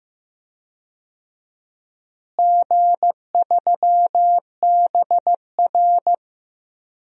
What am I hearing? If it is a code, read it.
Morse code, heard as a keyed tone, about 15 wpm, 700 Hz. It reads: G3BR
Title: DXCC-100 Call Signs - 4 Characters - Repeated 3x - 15wpm